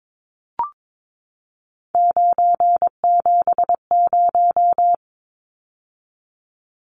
Morse code, heard as a keyed tone, about 22 wpm, 700 Hz. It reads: 970
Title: Numbers - 3-Digit 22wpm